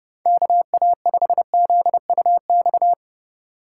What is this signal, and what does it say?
Morse code, heard as a keyed tone, about 30 wpm, 700 Hz. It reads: KA5ZUX